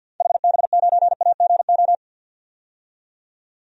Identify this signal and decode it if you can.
Morse code, heard as a keyed tone, about 47 wpm, 700 Hz. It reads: HB9AGO
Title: DXCC-100 Call Signs - 6 Characters - 50wpm